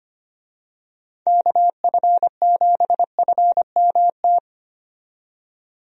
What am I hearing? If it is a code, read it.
Morse code, heard as a keyed tone, about 25 wpm, 700 Hz. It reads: KF7FMT
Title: USA Call Signs - 6 Characters - 25wpm